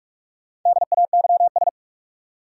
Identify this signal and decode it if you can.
Morse code, heard as a keyed tone, about 42 wpm, 700 Hz. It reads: DAYS
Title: Sentences from Top 1000 Words - Review Part 5 - New Word Drill - 45wpm